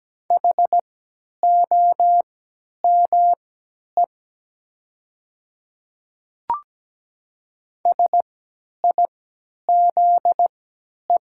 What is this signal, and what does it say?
Morse code, heard as a keyed tone, about 17 wpm, spaced as for 10 wpm, 700 Hz. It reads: HOME SIZE
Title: 3x Character Spacing - Top 500 Words - 17wpm